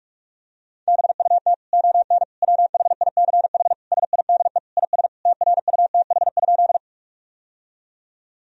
Morse code, heard as a keyed tone, about 46 wpm, 700 Hz. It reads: BUT ON WHICH SIDE IS TRUTH?
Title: Sentences from Top 1000 Words - Review Part 1 - 45wpm